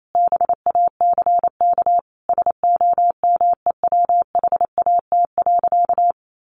Morse code, heard as a keyed tone, about 28 wpm, 700 Hz. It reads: BACK SOMEWHAT.
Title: Sentences from Top 1300 Words 28wpm